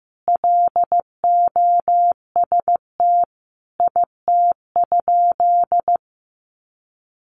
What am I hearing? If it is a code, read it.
Morse code, heard as a keyed tone, about 15 wpm, 700 Hz. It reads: LOST IT?
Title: Sentences from Top 1400 Words 15wpm